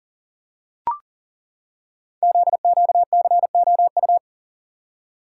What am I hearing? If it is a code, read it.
Morse code, heard as a keyed tone, about 39 wpm, 700 Hz. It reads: ZQCOU
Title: Code Groups - 5 Letters - 40wpm